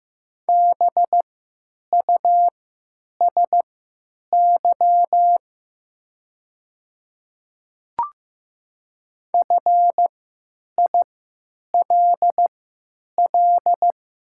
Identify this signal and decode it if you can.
Morse code, heard as a keyed tone, about 15 wpm, spaced as for 9 wpm, 700 Hz. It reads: BUSY FILL
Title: 3x Character Spacing - Top 500 Words - 15wpm